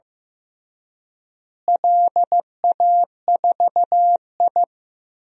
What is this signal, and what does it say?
Morse code, heard as a keyed tone, about 16 wpm, 700 Hz. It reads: LA4I
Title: DXCC-100 Call Signs - 4 Characters - Repeated 3x - 15wpm